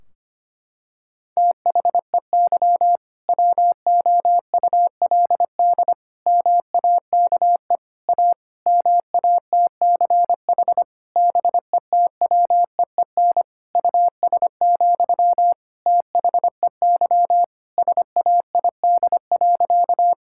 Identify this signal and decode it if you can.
Morse code, heard as a keyed tone, about 25 wpm, 700 Hz. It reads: THEY WOULD MAKE A MATCH BETWEEN US, THEY SAID.
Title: Sentences from Top 1600 Words 25wpm